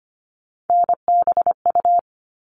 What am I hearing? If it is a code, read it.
Morse code, heard as a keyed tone, about 25 wpm, 700 Hz. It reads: NBU
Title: All Cut Numbers - 3-Digit 25wpm